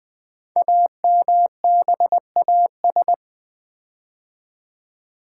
Morse code, heard as a keyed tone, about 20 wpm, 700 Hz. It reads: AMBAS
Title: Build A Word - 5 Letters At A Time - 10-Letter Words - 20wpm